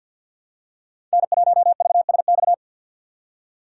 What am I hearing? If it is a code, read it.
Morse code, heard as a keyed tone, about 46 wpm, 700 Hz. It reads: N1VSX
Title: USA Call Signs - 5 Characters - 2x Character Spacing - 50wpm